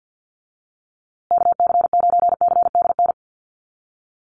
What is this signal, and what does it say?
Morse code, heard as a keyed tone, about 50 wpm, 700 Hz. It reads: KC9CDN
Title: USA Call Signs Repeated 2x 50wpm